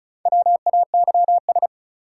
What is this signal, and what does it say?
Morse code, heard as a keyed tone, about 36 wpm, 700 Hz. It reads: WAYS
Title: Sets of 5 Words Encore - New Word Drill - 35wpm